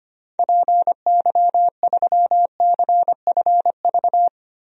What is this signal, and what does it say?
Morse code, heard as a keyed tone, about 25 wpm, 700 Hz. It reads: PY3CFV
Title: DXCC-100 Call Signs - 6 Characters - Repeated 2x - 25wpm